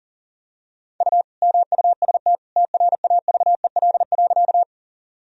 Morse code, heard as a keyed tone, about 40 wpm, 700 Hz. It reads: U MUST TRAVEL.
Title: Sentences from Top 400 Words and Abbreviations 40wpm